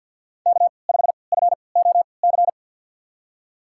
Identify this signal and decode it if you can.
Morse code, heard as a keyed tone, about 51 wpm, spaced as for 28 wpm, 700 Hz. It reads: K5FOC
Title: USA Call Signs - 5 Characters - 3x Character Spacing - 50wpm